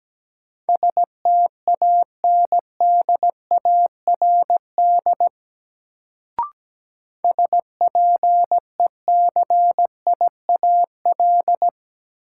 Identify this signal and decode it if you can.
Morse code, heard as a keyed tone, about 17 wpm, 700 Hz. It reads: STANDARD SPECIAL
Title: Sentences from Top 1200 Words - New Word Drill Alt - 17wpm